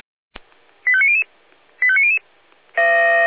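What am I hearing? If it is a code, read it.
Five-tone selective call (ZVEI), heard as quick runs of five tones, tones of about 80 ms. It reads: 85900 85900
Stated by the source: ZVEI five-tone selective call 85250 received on SDR, from the feuerwehr-orlamuende.org fire brigade audio server.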